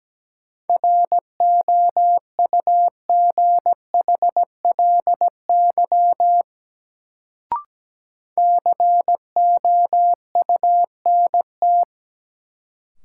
Morse code, heard as a keyed tone, about 17 wpm, 700 Hz. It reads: ROUGHLY COUNT
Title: Sentences from Top 1500 Words - New Word Drill - 17wpm